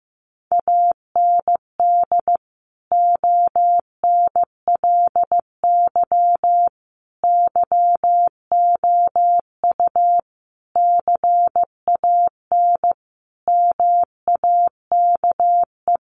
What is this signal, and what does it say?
Morse code, heard as a keyed tone, about 15 wpm, 700 Hz. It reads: AND ONLY YOU CAN MAKE
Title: Sentences from Top 1400 Words 15wpm